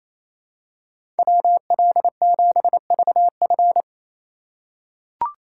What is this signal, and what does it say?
Morse code, heard as a keyed tone, about 28 wpm, 700 Hz. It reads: WL7VF
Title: USA Call Signs - 5 Characters - 4x Character Spacing - 28wpm